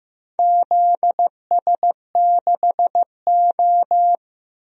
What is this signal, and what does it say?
Morse code, heard as a keyed tone, about 15 wpm, 700 Hz. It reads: ZS6O
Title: DXCC-100 Call Signs - 4 Characters - Repeated 3x - 15wpm